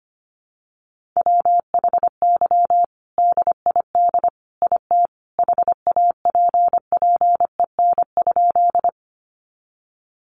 Morse code, heard as a keyed tone, about 25 wpm, 700 Hz. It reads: WHY DID IT HAPPEN?
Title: Sentences from Top 400 Words 25wpm